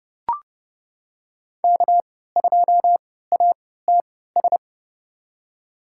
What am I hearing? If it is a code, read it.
Morse code, heard as a keyed tone, about 30 wpm, spaced as for 17 wpm, 700 Hz. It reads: K2ATS
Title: USA Call Signs - 3x Character Spacing - 30wpm